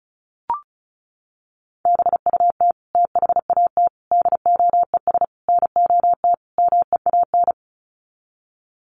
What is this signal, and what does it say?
Morse code, heard as a keyed tone, about 34 wpm, 700 Hz. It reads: BUT THAT DOES NOT MEAN
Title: Sets of 5 Words 35wpm